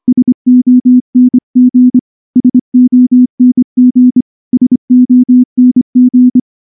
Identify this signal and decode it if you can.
Morse code, heard as a keyed tone, about 23 wpm, 250 Hz. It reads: SONG SONG SONG
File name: morse / morse-training-25WPM